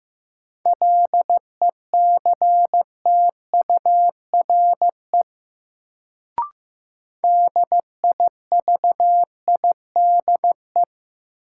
Morse code, heard as a keyed tone, about 15 wpm, 700 Hz. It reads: LECTURE DIVIDE
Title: Sentences from Top 1600 Words - New Word Drill - 15wpm